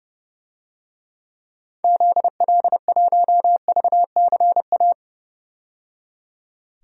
Morse code, heard as a keyed tone, about 30 wpm, 700 Hz. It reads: ZL1VCA
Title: DXCC-100 Call Signs - 6 Characters - 30wpm